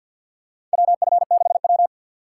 Morse code, heard as a keyed tone, about 47 wpm, 700 Hz. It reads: WF6W
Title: USA Call Signs - 4x Character Spacing - 50wpm